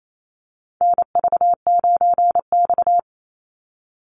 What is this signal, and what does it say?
Morse code, heard as a keyed tone, about 28 wpm, 700 Hz. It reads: NV9X
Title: Call Signs Repeated 2x - 28wpm